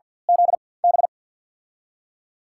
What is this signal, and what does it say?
Morse code, heard as a keyed tone, about 51 wpm, spaced as for 23 wpm, 700 Hz. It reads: ZB
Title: USA Call Signs - 6 Characters - 4x Character Spacing - 50wpm